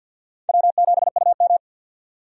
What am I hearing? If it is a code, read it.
Morse code, heard as a keyed tone, about 47 wpm, 700 Hz. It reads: W7UM
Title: USA Call Signs - 4 Characters - 50wpm